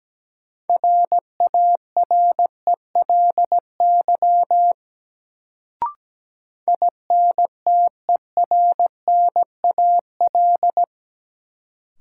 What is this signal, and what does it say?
Morse code, heard as a keyed tone, about 17 wpm, 700 Hz. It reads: RARELY INTERNAL
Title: Sentences from Top 1400 Words - New Word Drill - 17wpm